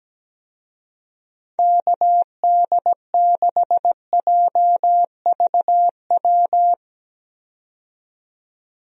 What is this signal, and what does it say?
Morse code, heard as a keyed tone, about 17 wpm, 700 Hz. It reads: KD6JVW
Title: USA Call Signs - 6 Characters - 17wpm